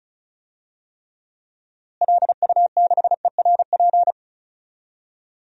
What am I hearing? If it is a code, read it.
Morse code, heard as a keyed tone, about 34 wpm, 700 Hz. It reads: LU6ERP
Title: DXCC-100 Call Signs - 6 Characters - Repeated 2x - 35wpm